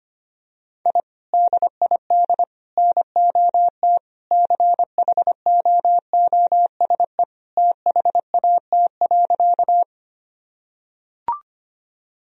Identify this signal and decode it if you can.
Morse code, heard as a keyed tone, about 25 wpm, 700 Hz. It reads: I DID NOT CHOOSE THAT.
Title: Sentences from Top 1000 Words - Review Part 4 - 25wpm